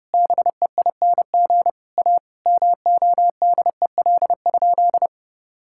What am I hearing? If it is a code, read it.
Morse code, heard as a keyed tone, about 32 wpm, 700 Hz. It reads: BEING A MODEL?
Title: Sentences from Top 1000 Words - Review Part 5 - 30wpm